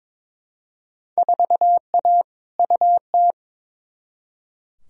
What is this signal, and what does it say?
Morse code, heard as a keyed tone, about 22 wpm, 700 Hz. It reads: 4A UT
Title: Field Day Exchanges - 22wpm